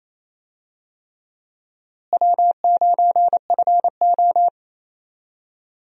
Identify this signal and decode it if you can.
Morse code, heard as a keyed tone, about 28 wpm, 700 Hz. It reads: W9FO
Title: USA Call Signs - 4 Characters - Repeated 3x - 28wpm